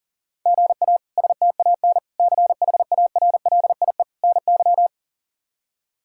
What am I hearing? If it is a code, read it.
Morse code, heard as a keyed tone, about 40 wpm, 700 Hz. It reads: GA STAN CHARLIE NY
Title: K1USN SST QSO Practice - 40wpm